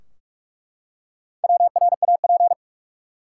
Morse code, heard as a keyed tone, about 41 wpm, 700 Hz. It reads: WRAP
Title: Four Letter Words - 45wpm